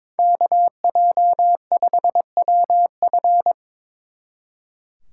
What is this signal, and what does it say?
Morse code, heard as a keyed tone, about 25 wpm, 700 Hz. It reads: KJ5WF
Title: Speed Racing - USA 2x2 Call Signs 22wpm